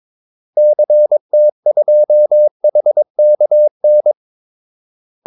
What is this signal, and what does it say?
Morse code, heard as a keyed tone, about 22 wpm, 600 Hz. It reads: CT2HKN
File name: Callsigns-22wpm